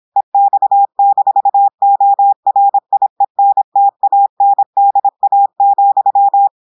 Morse code, heard as a keyed tone, about 26 wpm, 800 Hz. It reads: EX-ORIENTANDA,
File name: audios / portugais / portugais texte en clair 1 vitesse 26 mots minutes